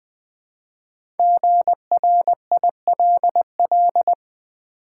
Morse code, heard as a keyed tone, about 20 wpm, 700 Hz. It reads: GRILL
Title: Easy Words with Hints - 20wpm